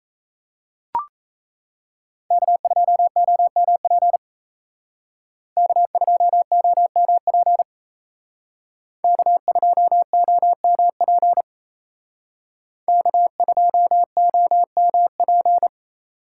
Morse code, heard as a keyed tone, about 37 wpm, 700 Hz. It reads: K2OMP K2OMP K2OMP K2OMP